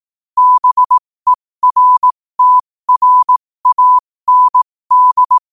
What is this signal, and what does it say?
Morse code, heard as a keyed tone, about 18 wpm, spaced as for 15 wpm, 1000 Hz. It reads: BERTRAND
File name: qotdfiles / 15wpm